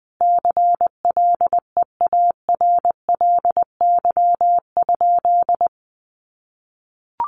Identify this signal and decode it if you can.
Morse code, heard as a keyed tone, about 20 wpm, 700 Hz. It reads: CLEARLY?
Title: Sentences from Top 1300 Words 20wpm